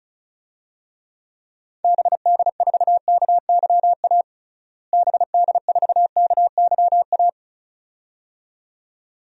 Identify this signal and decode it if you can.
Morse code, heard as a keyed tone, about 34 wpm, 700 Hz. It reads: BD4KYA BD4KYA
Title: DXCC-100 Call Signs - 6 Characters - Repeated 2x - 35wpm